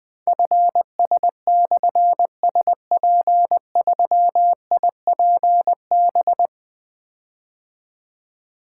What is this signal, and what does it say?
Morse code, heard as a keyed tone, about 20 wpm, 700 Hz. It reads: FS/SP3IPB
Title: Difficult Call Signs 20wpm